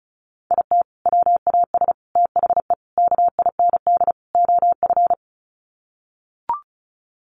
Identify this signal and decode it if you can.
Morse code, heard as a keyed tone, about 34 wpm, 700 Hz. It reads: IT WAS THE KIND OF
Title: Sets of 5 Words Encore2 35wpm